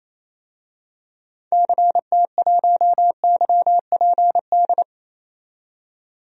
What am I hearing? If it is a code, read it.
Morse code, heard as a keyed tone, about 28 wpm, 700 Hz. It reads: CT1YPD
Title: DXCC-100 Call Signs - 6 Characters - 28wpm